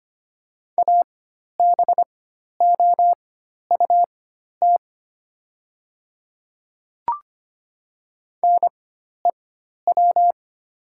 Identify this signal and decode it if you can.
Morse code, heard as a keyed tone, about 25 wpm, spaced as for 12 wpm, 700 Hz. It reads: ABOUT NEW
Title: 4x Character Spacing - Top 300 Words - 25wpm